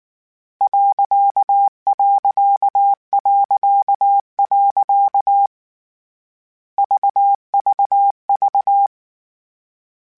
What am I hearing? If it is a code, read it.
Morse code, heard as a keyed tone, about 19 wpm, 800 Hz. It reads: .... VVV
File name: Misc / Misc-mp3s / Punct @18